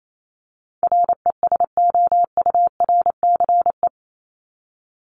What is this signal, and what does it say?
Morse code, heard as a keyed tone, about 28 wpm, 700 Hz. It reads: RESOURCE
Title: Top 1001-1250 Words 28wpm